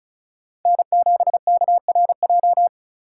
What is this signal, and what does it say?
Morse code, heard as a keyed tone, about 45 wpm, 700 Hz. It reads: N7KRJ